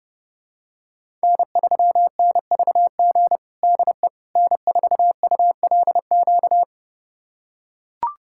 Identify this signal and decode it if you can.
Morse code, heard as a keyed tone, about 32 wpm, 700 Hz. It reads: N3NVG DE N4ULQ
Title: USA Callsign DE Callsign - 5 Characters - 30wpm